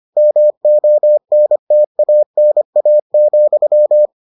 Morse code, heard as a keyed tone, about 25 wpm, 600 Hz. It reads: MONTANA,